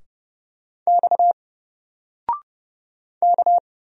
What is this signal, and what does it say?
Morse code, heard as a keyed tone, about 30 wpm, 700 Hz. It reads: X K
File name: Single-Letter-Number-30wpm